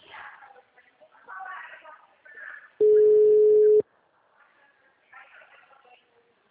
Ringback tone (Europe, 425 Hz).
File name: out- -2026- - - .3012